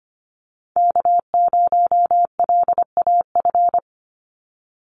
Morse code, heard as a keyed tone, about 25 wpm, 700 Hz. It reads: K0LAF
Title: NA SOTA Chaser Callsigns 25wpm